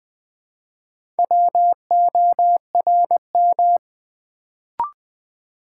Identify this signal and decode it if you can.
Morse code, heard as a keyed tone, about 20 wpm, 700 Hz. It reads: WORM
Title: Build A Word - 2 Letters At A Time - 4-Letter Words - 20wpm